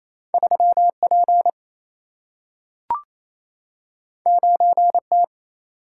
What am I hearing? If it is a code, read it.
Morse code, heard as a keyed tone, about 28 wpm, 700 Hz. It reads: 3P 9T
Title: Single-Letter Single-Number 28wpm